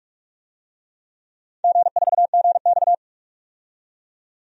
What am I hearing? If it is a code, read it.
Morse code, heard as a keyed tone, about 46 wpm, 700 Hz. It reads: G4GX
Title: DXCC-100 Call Signs - 4 Characters - 45wpm